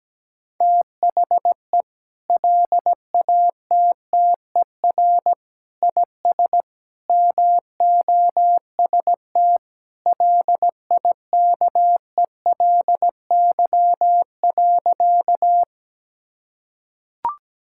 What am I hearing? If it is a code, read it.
Morse code, heard as a keyed tone, about 17 wpm, 700 Hz. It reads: THE LATTER IS MOST LIKELY.
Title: Sentences from Top 1600 Words 17wpm